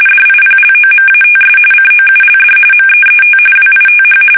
RTTY, about 45.45 baud. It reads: DIPLOMATS COULD MEET WITH
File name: fsk850 / usnavy1